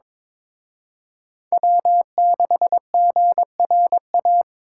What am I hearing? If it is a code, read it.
Morse code, heard as a keyed tone, about 26 wpm, 700 Hz. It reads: W6GRA
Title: Speed Racing - USA 1x3 Call Signs 22wpm